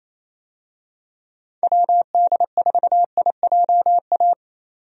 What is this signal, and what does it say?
Morse code, heard as a keyed tone, about 28 wpm, 700 Hz. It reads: WD4IJA
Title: USA Call Signs - 6 Characters - 28wpm